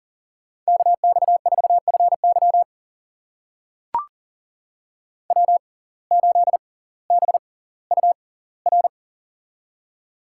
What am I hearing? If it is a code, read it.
Morse code, heard as a keyed tone, about 40 wpm, spaced as for 20 wpm, 700 Hz. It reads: KX4FY W8BUR
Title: USA Call Signs - 5 Characters - 6x Character Spacing - 40wpm